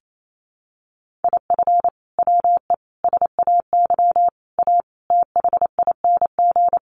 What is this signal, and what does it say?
Morse code, heard as a keyed tone, about 28 wpm, 700 Hz. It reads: IF WE SAY A THING
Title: Sentences from Top 1200 Words 28wpm